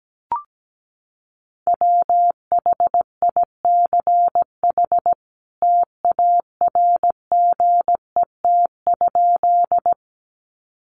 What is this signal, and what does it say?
Morse code, heard as a keyed tone, about 17 wpm, 700 Hz. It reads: WHICH TARGET?
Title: Sentences from Top 1200 Words 17wpm